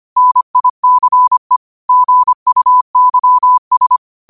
Morse code, heard as a keyed tone, about 25 wpm, 1000 Hz. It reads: NICE GUYS
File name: qotdfiles / 25wpm